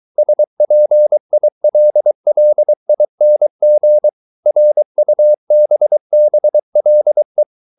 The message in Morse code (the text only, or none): SPILLING RUBBLE